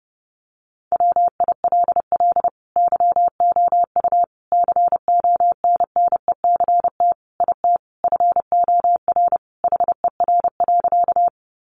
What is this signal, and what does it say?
Morse code, heard as a keyed tone, about 30 wpm, 700 Hz. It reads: WILL YOU CONNECT IT FOR HER.
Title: Sentences from Top 1500 Words 30wpm